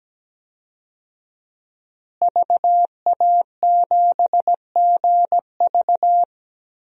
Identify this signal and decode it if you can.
Morse code, heard as a keyed tone, about 17 wpm, 700 Hz. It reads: VA7GV
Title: DXCC-100 Call Signs - 5 Characters - Repeated 2x - 17wpm